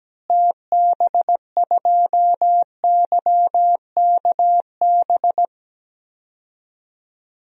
Morse code, heard as a keyed tone, about 17 wpm, 700 Hz. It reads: TB2YKB
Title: DXCC-100 Call Signs - 6 Characters - Repeated 2x - 17wpm